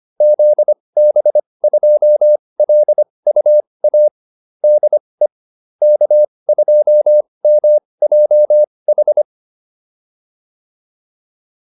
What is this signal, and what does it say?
Morse code, heard as a keyed tone, about 25 wpm, spaced as for 20 wpm, 600 Hz. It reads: ZB2LUA DE K2MJH